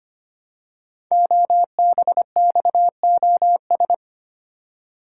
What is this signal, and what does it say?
Morse code, heard as a keyed tone, about 25 wpm, 700 Hz. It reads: OBXOS
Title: Code Groups - 5 Letters - 25wpm